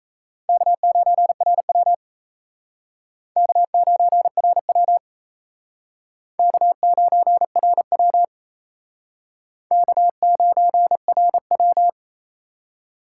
Morse code, heard as a keyed tone, about 36 wpm, 700 Hz. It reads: K9RW K9RW K9RW K9RW